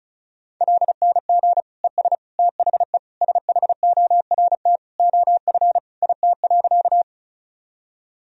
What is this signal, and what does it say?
Morse code, heard as a keyed tone, about 34 wpm, 700 Hz. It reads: LNG ES THE SHORT OF IT.
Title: Sentences from Top 400 Words and Abbreviations 35wpm